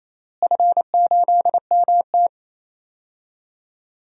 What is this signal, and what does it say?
Morse code, heard as a keyed tone, about 28 wpm, 700 Hz. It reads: F8MT
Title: DXCC-100 Call Signs - 4 Characters - Repeated 2x - 28wpm